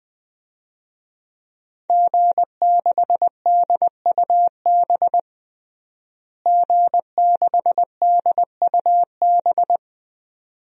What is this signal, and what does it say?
Morse code, heard as a keyed tone, about 20 wpm, 700 Hz. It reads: G6DUB G6DUB
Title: DXCC-100 Call Signs - 5 Characters - Repeated 2x - 20wpm